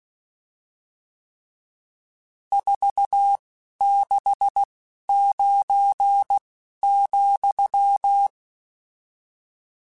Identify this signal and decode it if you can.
Morse code, heard as a keyed tone, about 16 wpm, 800 Hz. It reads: 4 6 9 ,
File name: RSTs / RST-mp3s / Various RST @10